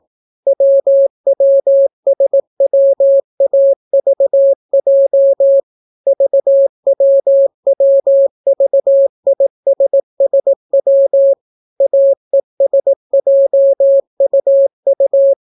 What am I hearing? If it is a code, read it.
Morse code, heard as a keyed tone, about 18 wpm, 550 Hz. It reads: WWSWAVJ VWWVISSW AESJUU